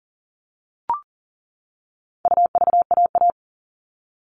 Morse code, heard as a keyed tone, about 40 wpm, 700 Hz. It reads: UVAA
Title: All Cut Numbers - 4-Digit 40wpm